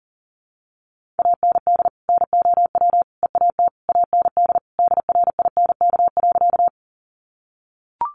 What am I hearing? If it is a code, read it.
Morse code, heard as a keyed tone, about 40 wpm, 700 Hz. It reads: AND NOW EAT AND DRINK.
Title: Sentences from Top 1500 Words 40wpm